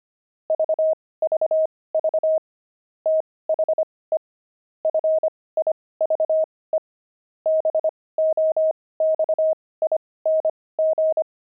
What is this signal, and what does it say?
Morse code, heard as a keyed tone, about 25 wpm, spaced as for 18 wpm, 630 Hz. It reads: VVV THE FIVE BOXING
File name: pangram-fs-18-cs-25-jumping